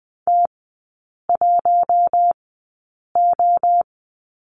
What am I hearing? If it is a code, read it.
Morse code, heard as a keyed tone, about 20 wpm, spaced as for 8 wpm, 700 Hz. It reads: T1O
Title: 133 Lesson 22e - Question Mark - Three Characters 20wpm